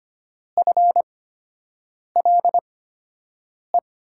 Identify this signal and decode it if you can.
Morse code, heard as a keyed tone, about 25 wpm, spaced as for 7 wpm, 700 Hz. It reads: FLE
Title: USA Call Signs - 6 Characters - 8x Character Spacing - 25wpm